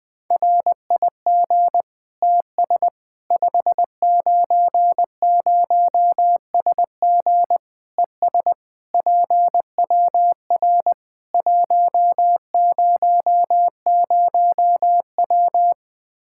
Morse code, heard as a keyed tone, about 20 wpm, 700 Hz. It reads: RIG TS 590SG ES PWR 100W
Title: Practice QSO - Exchange 2 - Info - 20wpm